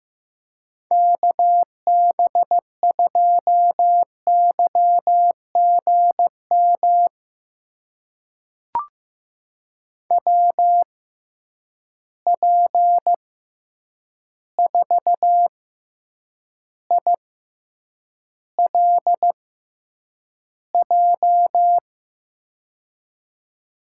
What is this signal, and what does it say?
Morse code, heard as a keyed tone, about 15 wpm, spaced as for 8 wpm, 700 Hz. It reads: KB2YGM WP4ILJ
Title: USA Call Signs - 6 Characters - 6x Character Spacing - 15wpm